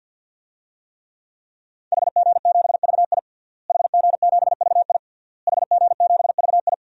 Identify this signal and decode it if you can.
Morse code, heard as a keyed tone, about 48 wpm, 700 Hz. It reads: HG7VI HG7VI HG7VI
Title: DXCC-100 Call Signs - 5 Characters - Repeated 3x - 50wpm